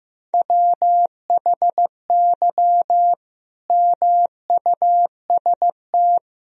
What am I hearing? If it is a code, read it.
Morse code, heard as a keyed tone, about 15 wpm, 700 Hz. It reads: WHY MUST
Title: Sentences from Top 1000 Words - Review Part 3 - 15wpm